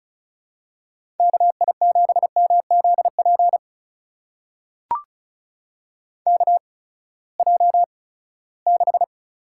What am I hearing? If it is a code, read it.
Morse code, heard as a keyed tone, about 36 wpm, spaced as for 20 wpm, 700 Hz. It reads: KI7MZP KJ6
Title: USA Call Signs - 8x Character Spacing - 35wpm